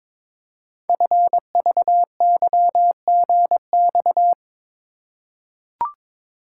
Morse code, heard as a keyed tone, about 22 wpm, 700 Hz. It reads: FVYGX
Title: Code Groups - 5 Letters - 22wpm